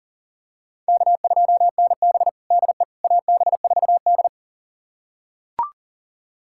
Morse code, heard as a keyed tone, about 40 wpm, 700 Hz. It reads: K2NB DE AB4D
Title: USA Callsign DE Callsign - 4 Characters - 40wpm